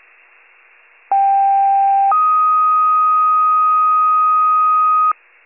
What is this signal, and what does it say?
Two-tone page, heard as a one-second tone, then a three-second tone, about 790 then 1250 Hz. It is heard from 1.1 to 5.1 s.